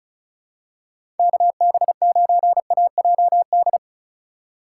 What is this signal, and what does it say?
Morse code, heard as a keyed tone, about 36 wpm, 700 Hz. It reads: KB9AJD
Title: USA Call Signs - 6 Characters - 2x Character Spacing - 35wpm